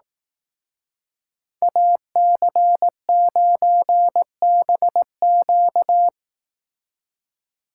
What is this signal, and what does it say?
Morse code, heard as a keyed tone, about 19 wpm, 700 Hz. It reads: AC9BQ